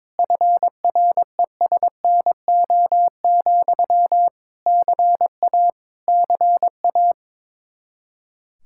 Morse code, heard as a keyed tone, about 22 wpm, 700 Hz. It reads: FRESNO, CA CA
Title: Practice QSO - Exchange 1 - Beginning - 22wpm